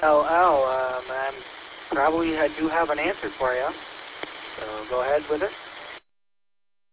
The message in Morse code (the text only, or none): none